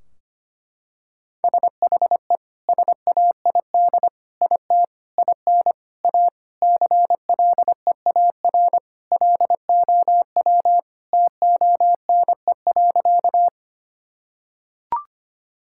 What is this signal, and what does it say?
Morse code, heard as a keyed tone, about 25 wpm, 700 Hz. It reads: SHE SAID IT IN A CLEAR LOW TONE.
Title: Sentences from Top 1600 Words 25wpm